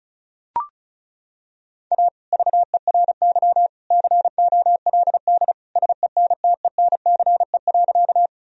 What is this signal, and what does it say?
Morse code, heard as a keyed tone, about 36 wpm, 700 Hz. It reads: A VERY COLD SENTENCE.
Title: Sentences from Top 1000 Words - Review Part 1 - 35wpm